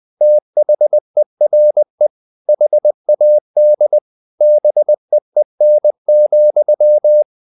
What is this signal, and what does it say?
Morse code, heard as a keyed tone, about 20 wpm, 600 Hz. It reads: THERE HAD BEEN,
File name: the old man and the sea part 2 20wpm